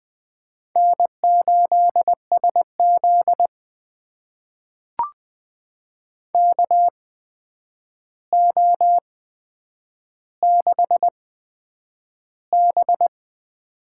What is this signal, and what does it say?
Morse code, heard as a keyed tone, about 20 wpm, spaced as for 9 wpm, 700 Hz. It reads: N8SZ KO6B
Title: USA Call Signs - 4 Characters - 8x Character Spacing - 20wpm